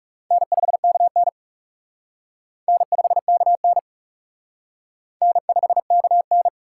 Morse code, heard as a keyed tone, about 36 wpm, 700 Hz. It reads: N5KN N5KN N5KN
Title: Speed Racing - USA 1x2 Call Signs 30wpm